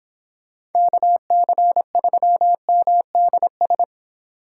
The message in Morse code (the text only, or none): KC3MDS